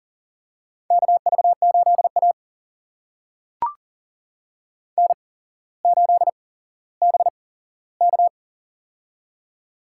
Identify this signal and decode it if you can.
Morse code, heard as a keyed tone, about 39 wpm, spaced as for 17 wpm, 700 Hz. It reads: KV8A N8BK
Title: USA Call Signs - 4 Characters - 8x Character Spacing - 40wpm